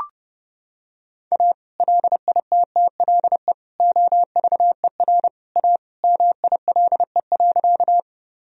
Morse code, heard as a keyed tone, about 28 wpm, 700 Hz. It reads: A LITTLE OVER A MILE.
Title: Sentences from Top 1000 Words - Review Part 1 - 30wpm